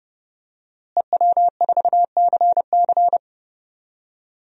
Morse code, heard as a keyed tone, about 30 wpm, 700 Hz. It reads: EW4CC
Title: DXCC-100 Call Signs - 5 Characters - Repeated 3x - 30wpm